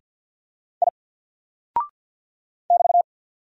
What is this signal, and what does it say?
Morse code, heard as a keyed tone, about 48 wpm, 700 Hz. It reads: I <BT>
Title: Single Character (Letter:Number:Punctuation:Prosign) 50wpm